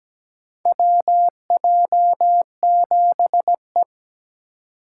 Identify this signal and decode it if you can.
Morse code, heard as a keyed tone, about 17 wpm, 700 Hz. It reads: WJ7E
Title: USA Call Signs - 4 Characters - 2x Character Spacing - 17wpm